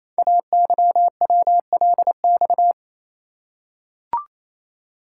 Morse code, heard as a keyed tone, about 28 wpm, 700 Hz. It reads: AYWLX
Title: Code Groups - 5 Letters - 28wpm